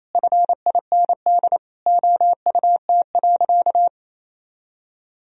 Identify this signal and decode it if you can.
Morse code, heard as a keyed tone, about 28 wpm, 700 Hz. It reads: FIND OUT.
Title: Sentences from Top 500 Words and Abbreviations 28wpm